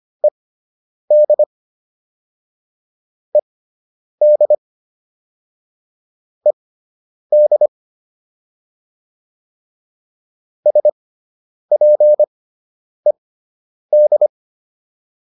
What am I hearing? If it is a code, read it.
Morse code, heard as a keyed tone, about 25 wpm, spaced as for 9 wpm, 600 Hz. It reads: ED ED ED SPED